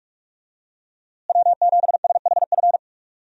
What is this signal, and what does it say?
Morse code, heard as a keyed tone, about 48 wpm, 700 Hz. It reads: W7SHF
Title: USA Call Signs - 5 Characters - Repeated 2x - 45wpm